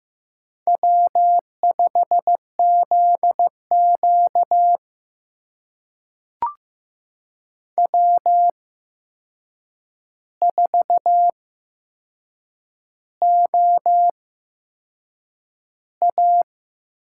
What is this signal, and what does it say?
Morse code, heard as a keyed tone, about 15 wpm, spaced as for 6 wpm, 700 Hz. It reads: W5ZQ W4OA
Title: USA Call Signs - 4 Characters - 8x Character Spacing - 15wpm